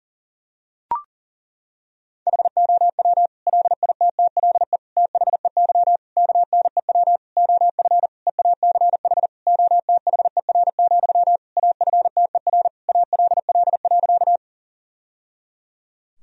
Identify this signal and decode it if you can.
Morse code, heard as a keyed tone, about 37 wpm, 700 Hz. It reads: HOW LITTLE THEY KNEW OF EACH OTHER, AFTER ALL.
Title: Sentences from Top 1000 Words - Review Part 1 - 40wpm